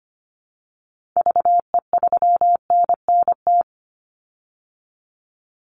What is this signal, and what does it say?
Morse code, heard as a keyed tone, about 25 wpm, 700 Hz. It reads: VE3NNT
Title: Call Signs Repeated 3x - 25wpm